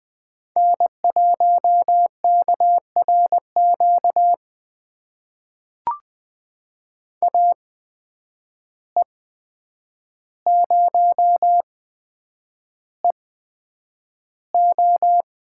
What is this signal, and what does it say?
Morse code, heard as a keyed tone, about 20 wpm, spaced as for 9 wpm, 700 Hz. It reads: N1KRQ AE0EO
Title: USA Call Signs - 5 Characters - 8x Character Spacing - 20wpm